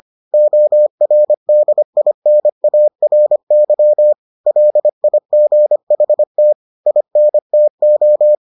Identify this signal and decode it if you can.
Morse code, heard as a keyed tone, about 25 wpm, 600 Hz. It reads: ORDINARY LIGHT INTO